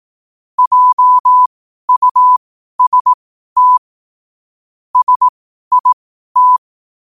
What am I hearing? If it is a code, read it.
Morse code, heard as a keyed tone, about 18 wpm, spaced as for 12 wpm, 1000 Hz. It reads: JUST SIT
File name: qotdfiles / 12wpm